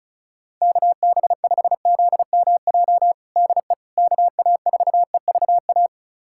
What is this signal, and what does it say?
Morse code, heard as a keyed tone, about 36 wpm, 700 Hz. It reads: KB5ZMJ DE KA4EVA
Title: USA Callsign DE Callsign - 6 Characters - 35wpm